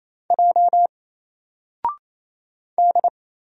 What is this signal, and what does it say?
Morse code, heard as a keyed tone, about 27 wpm, 700 Hz. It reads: J D
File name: Single-Letter-Number-28wpm